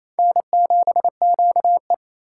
Morse code, heard as a keyed tone, about 28 wpm, 700 Hz. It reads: N7QE
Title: USA Call Signs - 4 Characters - 4x Character Spacing - 28wpm